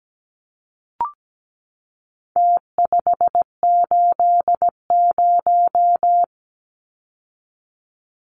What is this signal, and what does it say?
Morse code, heard as a keyed tone, about 17 wpm, 700 Hz. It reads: T580
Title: Numbers - Contest Style - 4-Digit 17wpm